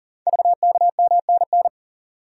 Morse code, heard as a keyed tone, about 39 wpm, 700 Hz. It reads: VKMNN
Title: Code Groups - 5 Letters - 40wpm